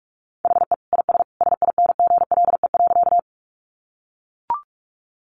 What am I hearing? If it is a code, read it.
Morse code, heard as a keyed tone, about 44 wpm, 700 Hz. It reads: HE IS SINGLE.
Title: Sentences from Top 700 Words 45wpm